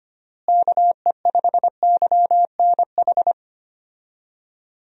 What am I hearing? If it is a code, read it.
Morse code, heard as a keyed tone, about 25 wpm, 700 Hz. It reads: KE5YNH
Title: USA Call Signs - 6 Characters - Repeated 3x - 25wpm